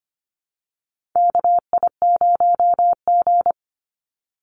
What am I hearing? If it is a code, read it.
Morse code, heard as a keyed tone, about 25 wpm, 700 Hz. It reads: KI0G
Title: NA SOTA Chaser Callsigns 25wpm